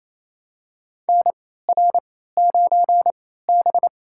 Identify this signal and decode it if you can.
Morse code, heard as a keyed tone, about 28 wpm, spaced as for 16 wpm, 700 Hz. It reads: NR9B
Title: USA Call Signs - 4 Characters - 3x Character Spacing - 28wpm